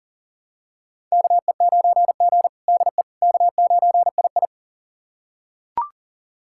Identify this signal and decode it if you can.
Morse code, heard as a keyed tone, about 37 wpm, 700 Hz. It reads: KE9G DE K9II
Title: USA Callsign DE Callsign - 4 Characters - 40wpm